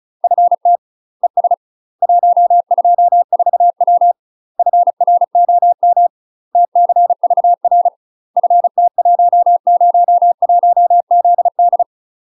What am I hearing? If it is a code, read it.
Morse code, heard as a keyed tone, about 36 wpm, 700 Hz. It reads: FT ES 124W FROM TCVR FT101ZD
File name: UK MASTER QSOs TRAINING / 35WPM / UK MASTER QSOs 0018